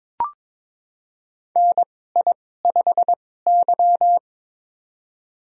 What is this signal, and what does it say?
Morse code, heard as a keyed tone, about 22 wpm, spaced as for 16 wpm, 700 Hz. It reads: NI5Y
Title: USA Call Signs - 2x Character Spacing - 22wpm